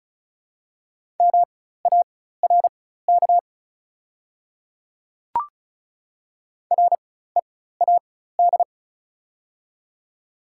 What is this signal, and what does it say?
Morse code, heard as a keyed tone, about 36 wpm, spaced as for 16 wpm, 690 Hz. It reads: MARK READ
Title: 4x Character Spacing - Top 300 Words - 35wpm